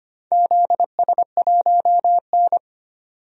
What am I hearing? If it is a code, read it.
Morse code, heard as a keyed tone, about 25 wpm, 700 Hz. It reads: ZS1N
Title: DXCC-100 Call Signs - 4 Characters - 25wpm